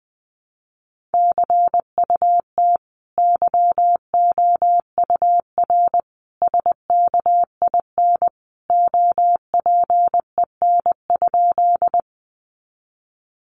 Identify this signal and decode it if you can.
Morse code, heard as a keyed tone, about 20 wpm, 700 Hz. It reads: CUT YOUR SKIN OPEN?
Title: Sentences from Top 1300 Words 20wpm